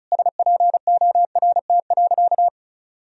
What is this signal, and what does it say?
Morse code, heard as a keyed tone, about 36 wpm, 700 Hz. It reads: SPORT.
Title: Sentences from Top 1000 Words - Review Part 4 - 35wpm